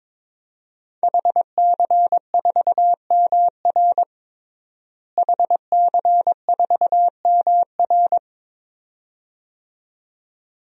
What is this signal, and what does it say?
Morse code, heard as a keyed tone, about 21 wpm, 700 Hz. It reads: HC4MR HC4MR
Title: DXCC-100 Call Signs - 5 Characters - Repeated 2x - 22wpm